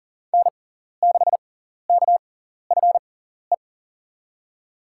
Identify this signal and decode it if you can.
Morse code, heard as a keyed tone, about 37 wpm, spaced as for 14 wpm, 700 Hz. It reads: N6KFE
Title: USA Call Signs - 5 Characters - 6x Character Spacing - 40wpm